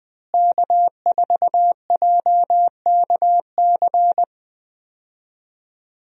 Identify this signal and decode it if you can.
Morse code, heard as a keyed tone, about 20 wpm, 700 Hz. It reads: K4JKC
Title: USA Call Signs - 5 Characters - Repeated 3x - 20wpm